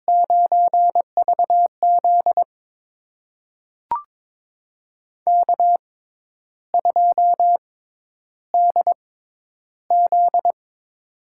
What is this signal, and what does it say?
Morse code, heard as a keyed tone, about 22 wpm, spaced as for 10 wpm, 700 Hz. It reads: 9VZ K2DZ
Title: USA Call Signs - 6x Character Spacing - 22wpm